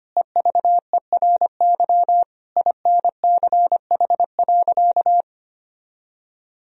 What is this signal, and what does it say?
Morse code, heard as a keyed tone, about 25 wpm, 700 Hz. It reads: EVERY INCH.
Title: Sentences from Top 1000 Words - Review Part 3 - 25wpm